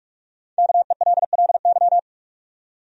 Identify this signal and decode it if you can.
Morse code, heard as a keyed tone, about 44 wpm, 700 Hz. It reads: KELLY
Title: USA Common Names - 45wpm